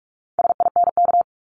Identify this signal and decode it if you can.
Morse code, heard as a keyed tone, about 46 wpm, 700 Hz. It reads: SINK
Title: Multi-letter Phonemes 45wpm